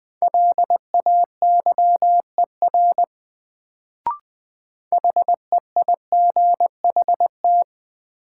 Morse code, heard as a keyed tone, about 20 wpm, 700 Hz. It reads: LAYER HEIGHT
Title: Sentences from Top 1600 Words - New Word Drill - 20wpm